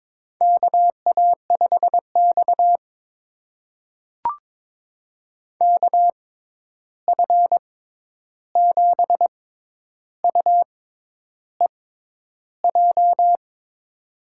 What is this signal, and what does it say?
Morse code, heard as a keyed tone, about 22 wpm, spaced as for 10 wpm, 700 Hz. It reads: KA5X KF7UEJ
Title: USA Call Signs - 6x Character Spacing - 22wpm